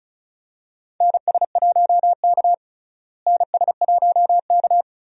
Morse code, heard as a keyed tone, about 34 wpm, 700 Hz. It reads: NS1K NS1K
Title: USA Call Signs - 4 Characters - Repeated 3x - 35wpm